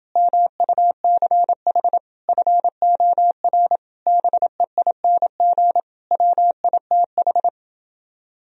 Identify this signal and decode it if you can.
Morse code, heard as a keyed tone, about 32 wpm, 700 Hz. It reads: MUCH FOR BEING WITH